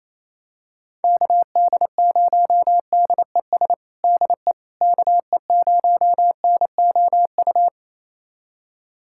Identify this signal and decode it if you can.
Morse code, heard as a keyed tone, about 28 wpm, 700 Hz. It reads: KD0DES DE KE0NOU
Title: USA Callsign DE Callsign - 6 Characters - 28wpm